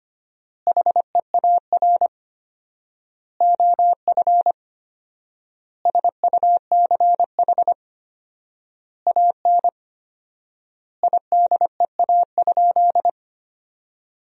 Morse code, heard as a keyed tone, about 25 wpm, 700 Hz. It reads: HEAR OF SUCH AN IDEA?
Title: 4x Word-Spacing - Sentences from Top 400 Words - 25wpm